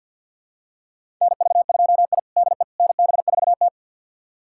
Morse code, heard as a keyed tone, about 51 wpm, 700 Hz. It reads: NV2I DE NB4T
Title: USA Callsign DE Callsign - 4 Characters - 50wpm